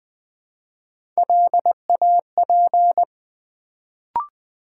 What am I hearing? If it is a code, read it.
Morse code, heard as a keyed tone, about 20 wpm, 700 Hz. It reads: LAP
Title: Build A Word - 1 Letter At A Time - 3-Letter Words - 20wpm